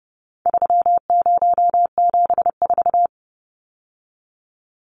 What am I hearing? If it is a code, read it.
Morse code, heard as a keyed tone, about 30 wpm, 700 Hz. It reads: 3074
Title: Numbers - 4-Digit 30wpm